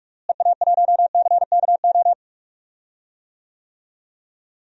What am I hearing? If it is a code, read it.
Morse code, heard as a keyed tone, about 42 wpm, 700 Hz. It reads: EA1CKO
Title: DXCC-100 Call Signs - 6 Characters - Repeated 2x - 45wpm